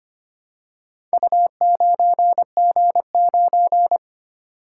Morse code, heard as a keyed tone, about 25 wpm, 700 Hz. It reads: U9G9
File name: practice / 25wpm / Four Letter-Number Combinations - 25wpm